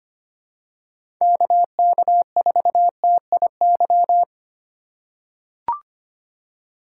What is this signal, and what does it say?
Morse code, heard as a keyed tone, about 25 wpm, 700 Hz. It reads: KK4TIY
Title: USA Call Signs - 4x Character Spacing - 25wpm